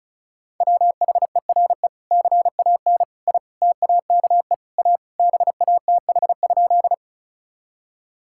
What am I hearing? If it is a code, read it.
Morse code, heard as a keyed tone, about 36 wpm, 700 Hz. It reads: WHERE CAN I TAKE A BATH?
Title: Sentences from Top 1600 Words 35wpm